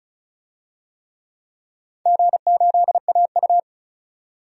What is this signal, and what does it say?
Morse code, heard as a keyed tone, about 34 wpm, 700 Hz. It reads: G8AU
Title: DXCC-100 Call Signs - 4 Characters - 35wpm